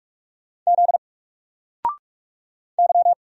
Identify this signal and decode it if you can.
Morse code, heard as a keyed tone, about 44 wpm, 700 Hz. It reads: Z Y
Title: Single Letter-Number 45wpm